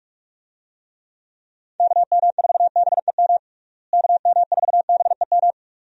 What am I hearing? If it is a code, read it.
Morse code, heard as a keyed tone, about 47 wpm, 700 Hz. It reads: KM4BEM KM4BEM
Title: USA Call Signs - 6 Characters - Repeated 2x - 45wpm